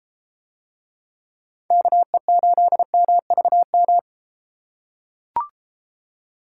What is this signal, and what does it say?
Morse code, heard as a keyed tone, about 33 wpm, 700 Hz. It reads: KE8MVM